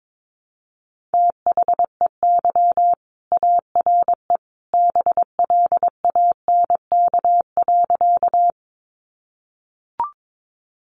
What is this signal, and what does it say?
Morse code, heard as a keyed tone, about 22 wpm, 700 Hz. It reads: THEY ARE BLANK.
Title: Sentences from Top 1500 Words 22wpm